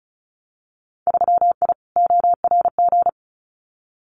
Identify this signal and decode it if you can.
Morse code, heard as a keyed tone, about 34 wpm, 700 Hz. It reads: 3I ORG
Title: Winter Field Day Exchanges Encore 35wpm